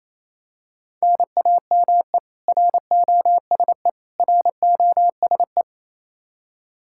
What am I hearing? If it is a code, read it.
Morse code, heard as a keyed tone, about 28 wpm, 700 Hz. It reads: NAME ROSE ROSE
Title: Practice QSO - Exchange 1 - Beginning - 28wpm